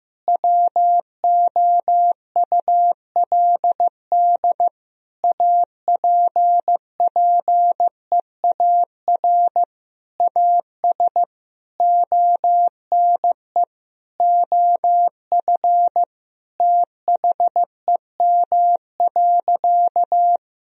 Morse code, heard as a keyed tone, about 15 wpm, 700 Hz. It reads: WOULD APPEAR AS ONE OF THEM.
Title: Sentences from Top 1000 Words - Review Part 2 - 15wpm